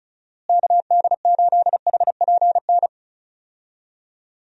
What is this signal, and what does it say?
Morse code, heard as a keyed tone, about 36 wpm, 700 Hz. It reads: KD8HPN
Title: USA Call Signs - 6 Characters - Repeated 3x - 35wpm